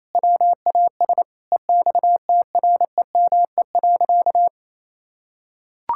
Morse code, heard as a keyed tone, about 28 wpm, 700 Hz. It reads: WAS EXTREME.
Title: Sentences from Top 1600 Words 28wpm